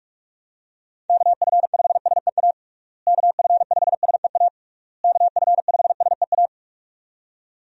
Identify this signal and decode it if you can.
Morse code, heard as a keyed tone, about 42 wpm, 700 Hz. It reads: KF5SEA KF5SEA KF5SEA
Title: USA Call Signs - 6 Characters - Repeated 3x - 45wpm